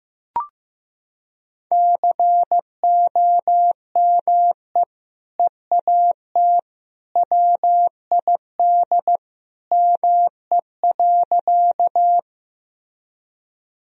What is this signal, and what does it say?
Morse code, heard as a keyed tone, about 15 wpm, 700 Hz. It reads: COME EAT WID ME.
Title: Sentences from Top 300 Words and Abbreviations 15wpm